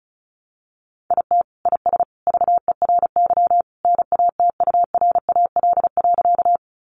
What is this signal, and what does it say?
Morse code, heard as a keyed tone, about 34 wpm, 700 Hz. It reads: IT IS VERY NATURAL.
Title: Sentences from Top 700 Words 35wpm